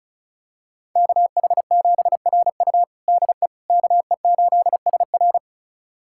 Morse code, heard as a keyed tone, about 34 wpm, 700 Hz. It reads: KH7RU DE KE8SR
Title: USA Callsign DE Callsign - 5 Characters - 35wpm